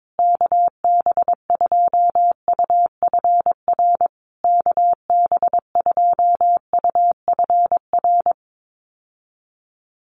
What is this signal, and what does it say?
Morse code, heard as a keyed tone, about 22 wpm, 700 Hz. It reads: KB2UFR KB2UFR
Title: USA Call Signs Repeated 3x 22wpm